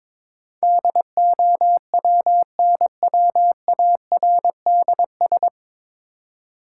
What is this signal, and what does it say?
Morse code, heard as a keyed tone, about 22 wpm, 700 Hz. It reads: DOWNWARDS
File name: 9 Letter Words - First 3 Letters Given - 22wpm